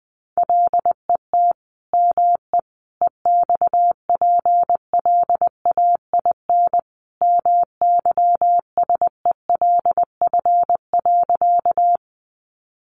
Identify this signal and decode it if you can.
Morse code, heard as a keyed tone, about 20 wpm, 700 Hz. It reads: LET ME EXPLAIN MYSELF.
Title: Sentences from Top 1300 Words 20wpm